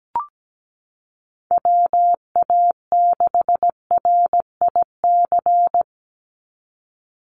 Morse code, heard as a keyed tone, about 17 wpm, 700 Hz. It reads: WA6RIC
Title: NA SOTA Chaser Callsigns 17wpm